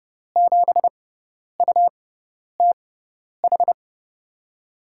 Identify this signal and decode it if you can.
Morse code, heard as a keyed tone, about 32 wpm, spaced as for 10 wpm, 700 Hz. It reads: 7UTH
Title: USA Call Signs - 6 Characters - 6x Character Spacing - 30wpm